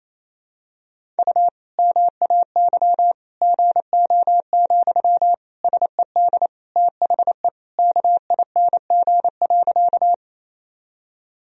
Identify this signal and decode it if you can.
Morse code, heard as a keyed tone, about 28 wpm, 700 Hz. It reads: U MAY GO, SED THE KING.
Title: Sentences from Top 400 Words and Abbreviations 28wpm